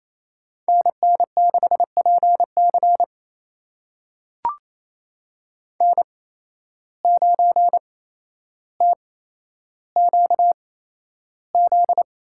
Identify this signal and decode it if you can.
Morse code, heard as a keyed tone, about 28 wpm, spaced as for 12 wpm, 700 Hz. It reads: NN6PC N9TQZ
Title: USA Call Signs - 5 Characters - 8x Character Spacing - 28wpm